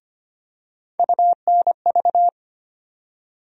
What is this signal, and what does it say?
Morse code, heard as a keyed tone, about 25 wpm, 700 Hz. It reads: UNV
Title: USA Call Sign Postfixes - 3 Letters 25wpm